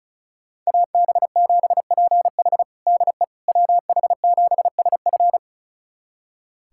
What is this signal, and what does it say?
Morse code, heard as a keyed tone, about 36 wpm, 700 Hz. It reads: AB7PH DE WH7SF
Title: USA Callsign DE Callsign - 5 Characters - 35wpm